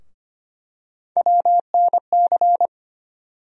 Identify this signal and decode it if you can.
Morse code, heard as a keyed tone, about 25 wpm, 700 Hz. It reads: WNC
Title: USA Call Sign Postfixes - 3 Letters 25wpm